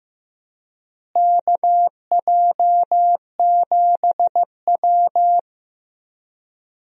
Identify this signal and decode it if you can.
Morse code, heard as a keyed tone, about 15 wpm, 700 Hz. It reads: KJ7W
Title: USA Call Signs - 4 Characters - Repeated 2x - 15wpm